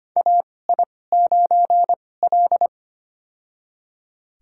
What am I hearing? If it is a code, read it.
Morse code, heard as a keyed tone, about 25 wpm, spaced as for 18 wpm, 700 Hz. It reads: AI9L
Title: USA Call Signs - 2x Character Spacing - 25wpm